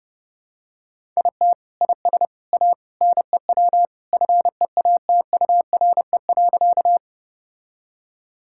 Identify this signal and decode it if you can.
Morse code, heard as a keyed tone, about 32 wpm, 700 Hz. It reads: IT IS A NEW FEATURE.
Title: Sentences from Top 1000 Words - Review Part 5 - 30wpm